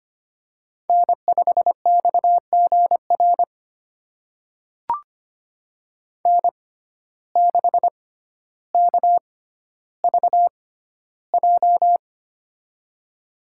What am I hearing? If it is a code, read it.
Morse code, heard as a keyed tone, about 25 wpm, spaced as for 13 wpm, 700 Hz. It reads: N5XGR N6KVJ
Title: USA Call Signs - 5 Characters - 6x Character Spacing - 25wpm